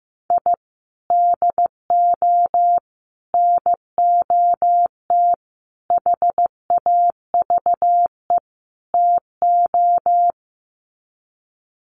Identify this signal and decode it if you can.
Morse code, heard as a keyed tone, about 15 wpm, 700 Hz. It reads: I DO NOT HAVE TO
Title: Sets of 5 Words 15wpm